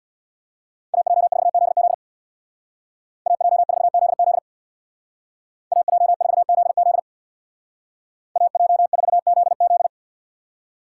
Morse code, heard as a keyed tone, about 64 wpm, 700 Hz. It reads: AJ4ZZ AJ4ZZ AJ4ZZ AJ4ZZ